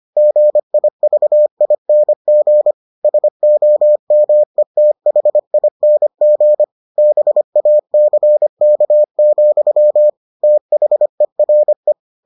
Morse code, heard as a keyed tone, about 25 wpm, 600 Hz. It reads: GIVING SOMETHING BACK, THERE